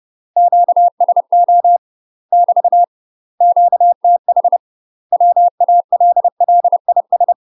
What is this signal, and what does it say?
Morse code, heard as a keyed tone, about 30 wpm, 700 Hz. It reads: QSO <BT> QTH WALLIS
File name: UK MASTER QSOs TRAINING / 30WPM / UK MASTER QSOs 0023